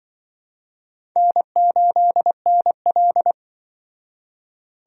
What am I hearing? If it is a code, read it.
Morse code, heard as a keyed tone, about 24 wpm, 700 Hz. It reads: N8NL
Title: Speed Racing - USA 1x2 Call Signs 20wpm